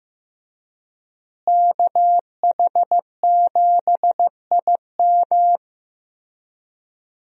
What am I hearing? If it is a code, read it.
Morse code, heard as a keyed tone, about 15 wpm, 700 Hz. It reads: KH7IM
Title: USA Call Signs - 5 Characters - Repeated 3x - 15wpm